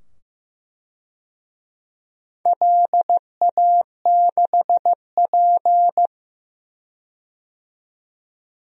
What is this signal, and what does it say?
Morse code, heard as a keyed tone, about 15 wpm, 700 Hz. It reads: LA6P
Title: DXCC-100 Call Signs - 4 Characters - 15wpm